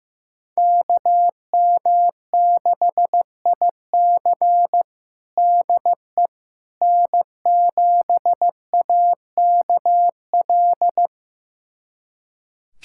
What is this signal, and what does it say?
Morse code, heard as a keyed tone, about 15 wpm, 700 Hz. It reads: KM6IC DE N7AKL
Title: USA Callsign DE Callsign - 5 Characters - 15wpm